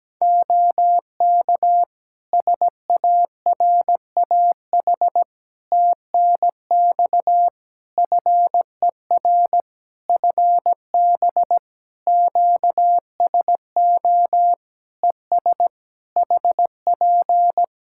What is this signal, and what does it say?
Morse code, heard as a keyed tone, about 17 wpm, 700 Hz. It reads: OK SARAH TNX FER FB QSO ES HP
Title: Practice QSO - Exchange 4 - Ending - 17wpm